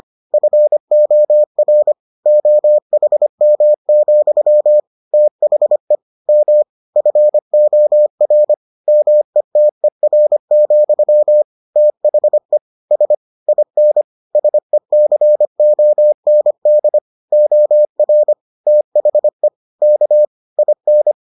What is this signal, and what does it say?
Morse code, heard as a keyed tone, about 25 wpm, 600 Hz. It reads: FOR OHM, THE M FOR METER, THE S IN SECOND OR THE K IN